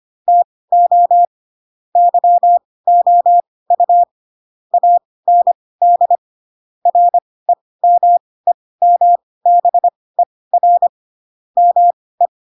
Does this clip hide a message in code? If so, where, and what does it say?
Yes: Morse code from 0.3 to 12.3 s, heard as a keyed tone, about 25 wpm, spaced as for 18 wpm, 700 Hz. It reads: TO YOU AND REMEMBER ME